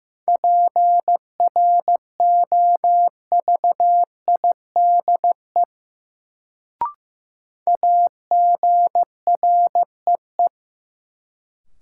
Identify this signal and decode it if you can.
Morse code, heard as a keyed tone, about 15 wpm, 700 Hz. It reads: PROVIDE AGREE
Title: Sentences from Top 1100 Words - New Word Drill Alt - 15wpm